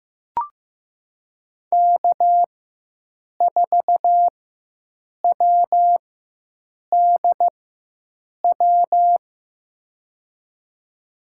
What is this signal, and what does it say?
Morse code, heard as a keyed tone, about 15 wpm, spaced as for 7 wpm, 700 Hz. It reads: K4WDW
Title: USA Call Signs - 5 Characters - 4x Character Spacing - 15wpm